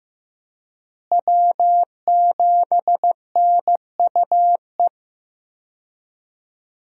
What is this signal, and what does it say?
Morse code, heard as a keyed tone, about 15 wpm, 700 Hz. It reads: W7NUE
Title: USA Call Signs - 5 Characters - Repeated 2x - 15wpm